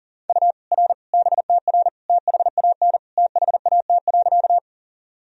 Morse code, heard as a keyed tone, about 37 wpm, 700 Hz. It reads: U R BTR THAN THAT.
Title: Sentences from Top 400 Words and Abbreviations 40wpm